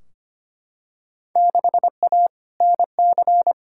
Morse code, heard as a keyed tone, about 25 wpm, 700 Hz. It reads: 6A NC
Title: Field Day Exchanges Encore - 25wpm